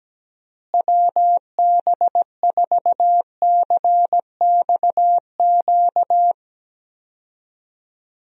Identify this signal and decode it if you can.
Morse code, heard as a keyed tone, about 17 wpm, 700 Hz. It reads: WB4CXQ
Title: USA Call Signs - 6 Characters - Repeated 2x - 17wpm